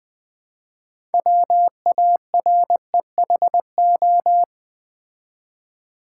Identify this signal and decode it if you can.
Morse code, heard as a keyed tone, about 20 wpm, 700 Hz. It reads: WAREHO
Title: Build A Word - 3 Letters At A Time - 9-Letter Words - 20wpm